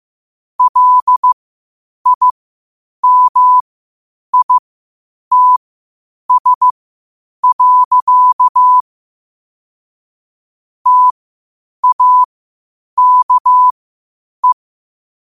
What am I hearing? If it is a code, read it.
Morse code, heard as a keyed tone, about 15 wpm, spaced as for 8 wpm, 1000 Hz. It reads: LIMITS. TAKE
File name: qotdfiles / 8wpm